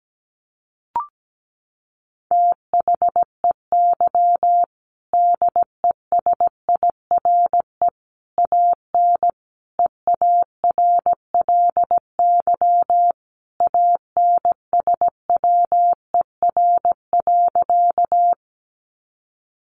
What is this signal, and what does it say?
Morse code, heard as a keyed tone, about 17 wpm, 700 Hz. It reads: THEY DESIRE AN EARLY ANSWER.
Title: Sentences from Top 1100 Words 17wpm